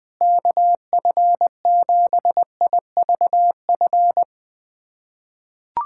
Morse code, heard as a keyed tone, about 20 wpm, 700 Hz. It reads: KF7IVF
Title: USA Callsign DE Callsign 20wpm